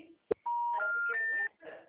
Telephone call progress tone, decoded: special information tone

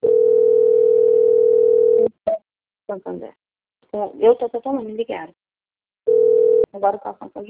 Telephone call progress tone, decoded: ringback tone

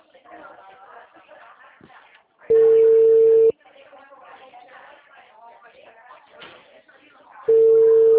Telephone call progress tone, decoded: ringback tone

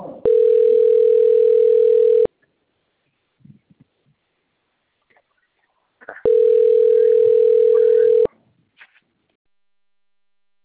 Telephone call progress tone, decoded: ringback tone

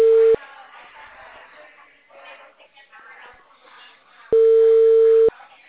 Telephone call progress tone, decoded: ringback tone

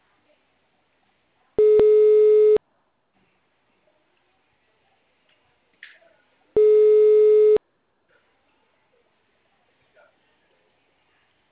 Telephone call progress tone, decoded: ringback tone

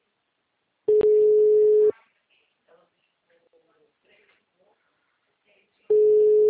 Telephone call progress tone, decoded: ringback tone

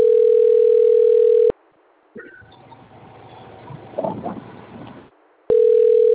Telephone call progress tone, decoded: ringback tone

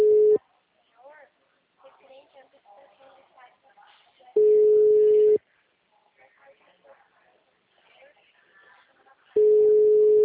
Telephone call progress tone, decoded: ringback tone

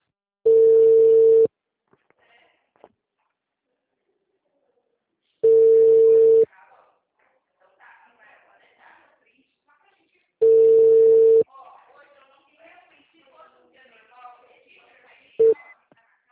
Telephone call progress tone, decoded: ringback tone

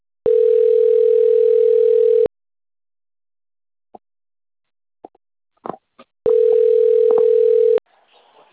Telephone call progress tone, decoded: ringback tone

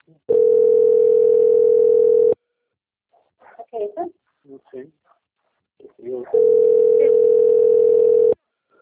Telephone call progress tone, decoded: ringback tone

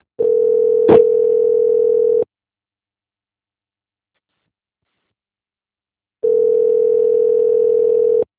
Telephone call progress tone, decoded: ringback tone

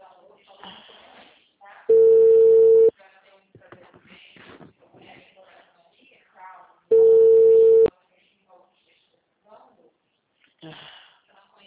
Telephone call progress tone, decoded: ringback tone

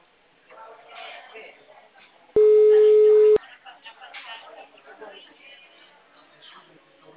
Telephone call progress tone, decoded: ringback tone

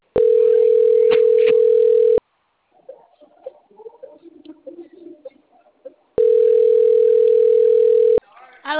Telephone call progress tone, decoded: ringback tone

